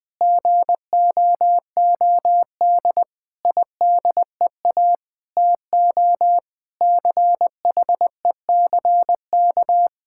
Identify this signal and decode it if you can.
Morse code, heard as a keyed tone, about 21 wpm, 700 Hz. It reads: GOOD IDEA TO CHECK
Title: Sentences with 6-Word Phrases - 20wpm